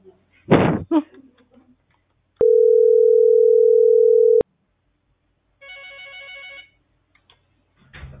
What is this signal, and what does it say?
Ringback tone (North America), 440 + 480 Hz.